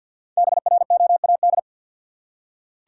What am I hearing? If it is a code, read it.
Morse code, heard as a keyed tone, about 47 wpm, 700 Hz. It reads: BROAD
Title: Five Letter Words - 50wpm